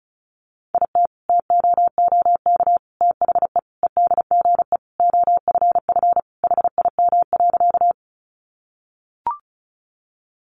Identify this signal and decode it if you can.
Morse code, heard as a keyed tone, about 34 wpm, 700 Hz. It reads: IT TOOK THE EDGE OFF HIM.
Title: Sentences from Top 1300 Words 35wpm